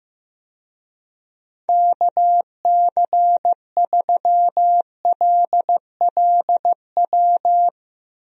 Morse code, heard as a keyed tone, about 15 wpm, 700 Hz. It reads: KC3LLW
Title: USA Call Signs - 6 Characters - Repeated 3x - 15wpm